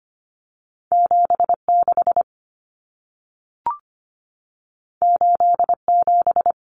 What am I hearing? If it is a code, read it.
Morse code, heard as a keyed tone, about 25 wpm, 700 Hz. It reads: 76 87
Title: Common Cut Numbers - 2-Digit 25wpm